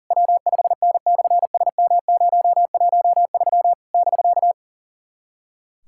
Morse code, heard as a keyed tone, about 40 wpm, 690 Hz. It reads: W5N/SM013 <BK>
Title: SOTA QSO Practice 40wpm